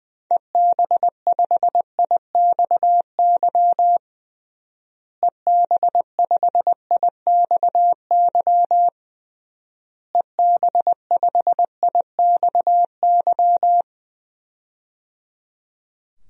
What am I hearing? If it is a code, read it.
Morse code, heard as a keyed tone, about 21 wpm, 700 Hz. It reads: EB5IXY EB5IXY EB5IXY
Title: DXCC-100 Call Signs - 6 Characters - Repeated 3x - 20wpm